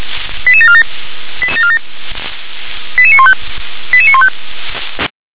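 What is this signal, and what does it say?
Five-tone selective call (ZVEI), heard as quick runs of five tones, tones of about 70 ms. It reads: 88747 88915 88915
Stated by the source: ZVEI five-tone call received by SDR.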